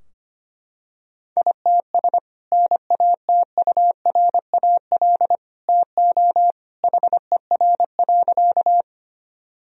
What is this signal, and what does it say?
Morse code, heard as a keyed tone, about 25 wpm, 700 Hz. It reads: ITS NATURAL TO HER.
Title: Sentences from Top 1000 Words - Review Part 4 - 25wpm